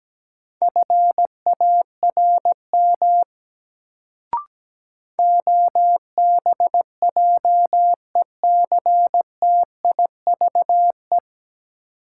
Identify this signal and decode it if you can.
Morse code, heard as a keyed tone, about 17 wpm, 700 Hz. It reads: FARM OBJECTIVE
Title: Sentences from Top 1200 Words - New Word Drill Alt - 17wpm